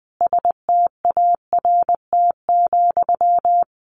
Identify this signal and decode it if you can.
Morse code, heard as a keyed tone, about 19 wpm, 700 Hz. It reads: START,
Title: Sentences from Top 300 Words 20wpm